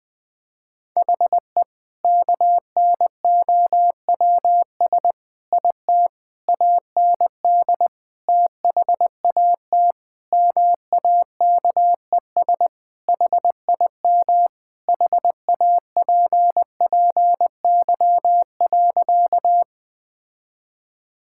Morse code, heard as a keyed tone, about 19 wpm, 700 Hz. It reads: HE KNOWS IT AND THAT MAKES HIM HAPPY.
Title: Sentences with 2-Word Phrases - 20wpm